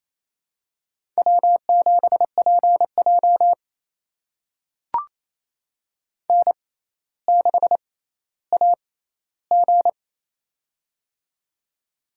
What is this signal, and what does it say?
Morse code, heard as a keyed tone, about 28 wpm, spaced as for 14 wpm, 700 Hz. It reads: W7PJ N6AG
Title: USA Call Signs - 4 Characters - 6x Character Spacing - 28wpm